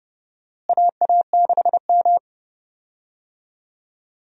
Morse code, heard as a keyed tone, about 30 wpm, 700 Hz. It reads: AA6M
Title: USA Call Signs - 4 Characters - Repeated 3x - 30wpm